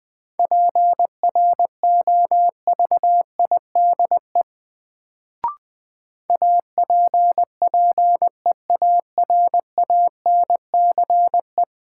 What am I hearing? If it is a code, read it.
Morse code, heard as a keyed tone, about 20 wpm, 700 Hz. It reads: PROVIDE APPEARANCE
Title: Sentences from Top 1100 Words - New Word Drill Alt - 20wpm